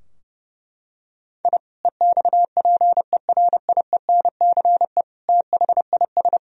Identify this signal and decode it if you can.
Morse code, heard as a keyed tone, about 30 wpm, 700 Hz. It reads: I EXPERIENCE THIS
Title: Sentences from Top 1000 Words - Review Part 3 - 30wpm